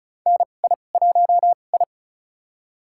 Morse code, heard as a keyed tone, about 36 wpm, spaced as for 25 wpm, 700 Hz. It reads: NI1I
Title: USA Call Signs - 4 Characters - 2x Character Spacing - 35wpm